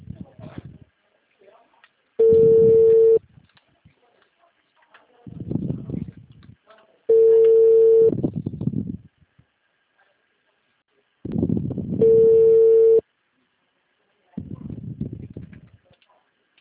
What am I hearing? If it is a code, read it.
Ringback tone (China, 450 Hz), with speech.